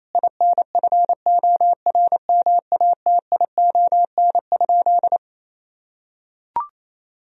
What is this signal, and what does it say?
Morse code, heard as a keyed tone, about 28 wpm, 700 Hz. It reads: INFORMATION?
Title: Sentences from Top 1000 Words - Review Part 3 - 28wpm